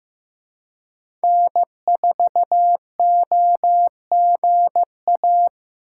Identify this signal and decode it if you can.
Morse code, heard as a keyed tone, about 15 wpm, 700 Hz. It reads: N4OGA
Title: USA Call Signs - 5 Characters - 2x Character Spacing - 15wpm